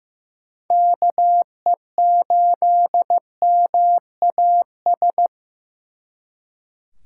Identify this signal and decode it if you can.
Morse code, heard as a keyed tone, about 18 wpm, 700 Hz. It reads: KE8MAS
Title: Speed Racing - USA 2x3 Call Signs 15wpm